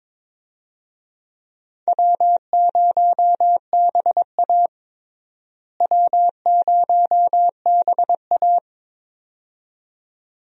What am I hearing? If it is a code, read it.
Morse code, heard as a keyed tone, about 22 wpm, 700 Hz. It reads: W0BA W0BA
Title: USA Call Signs - 4 Characters - Repeated 2x - 22wpm